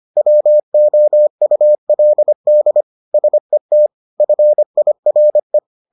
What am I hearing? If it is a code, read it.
Morse code, heard as a keyed tone, about 25 wpm, 600 Hz. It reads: WOULD SET FIRE